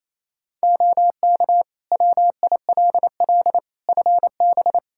Morse code, heard as a keyed tone, about 28 wpm, 700 Hz. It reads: OK WILL FB
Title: Practice QSO - Exchange 2 - Info - 28wpm